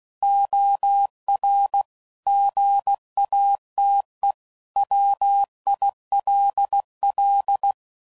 Morse code, heard as a keyed tone, about 16 wpm, 800 Hz. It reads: OR GATE WILL
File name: CW Code Groups @15WPM / Technical Subjects / OR Gate